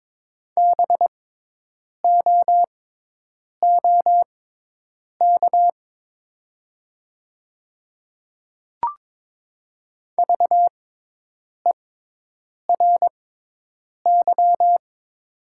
Morse code, heard as a keyed tone, about 22 wpm, spaced as for 8 wpm, 700 Hz. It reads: BOOK VERY
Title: 6x Character Spacing - Top 300 Words - 22wpm